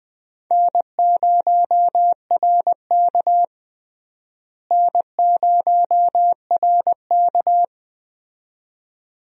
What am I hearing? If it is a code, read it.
Morse code, heard as a keyed tone, about 20 wpm, 700 Hz. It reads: N0RK N0RK
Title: USA Call Signs - 4 Characters - Repeated 2x - 20wpm